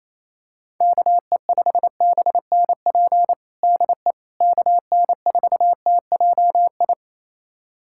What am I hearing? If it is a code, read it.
Morse code, heard as a keyed tone, about 28 wpm, 700 Hz. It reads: KE5BNP DE KN4TJI
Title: USA Callsign DE Callsign 28wpm